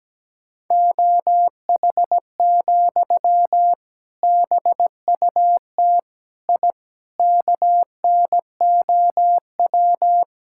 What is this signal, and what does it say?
Morse code, heard as a keyed tone, about 17 wpm, 700 Hz. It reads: OH, BUT I KNOW
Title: Sentences from Top 1000 Words - Review Part 6 - 17wpm